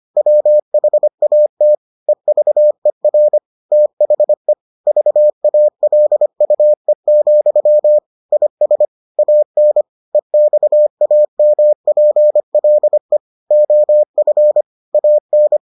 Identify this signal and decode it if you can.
Morse code, heard as a keyed tone, about 25 wpm, 600 Hz. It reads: WHATEVER THE VALUE, IS AN EXAMPLE OF AN